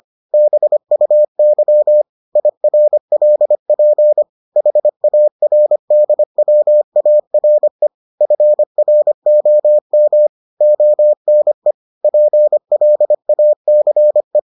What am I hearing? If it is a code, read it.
Morse code, heard as a keyed tone, about 25 wpm, 600 Hz. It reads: BUY IRLP HARDWARE FROM ONE PLACE